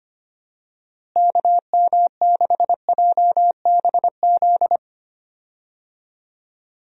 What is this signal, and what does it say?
Morse code, heard as a keyed tone, about 25 wpm, 700 Hz. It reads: KM6JBZ
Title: USA Call Signs - 6 Characters - Repeated 2x - 25wpm